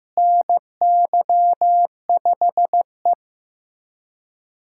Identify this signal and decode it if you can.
Morse code, heard as a keyed tone, about 15 wpm, 700 Hz. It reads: NY5E
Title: USA Callsign DE Callsign - 4 Characters - 15wpm